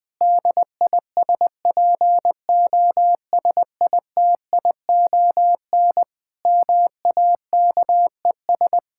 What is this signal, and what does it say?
Morse code, heard as a keyed tone, about 20 wpm, 700 Hz. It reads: DISPOSITION MAKES
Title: Difficult Sentences with Hints - 20wpm